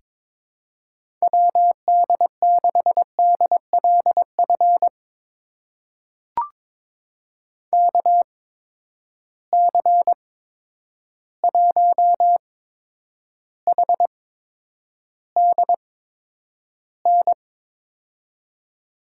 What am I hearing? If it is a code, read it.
Morse code, heard as a keyed tone, about 22 wpm, spaced as for 9 wpm, 700 Hz. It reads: WD6DLF KC1HDN
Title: USA Call Signs - 6 Characters - 8x Character Spacing - 22wpm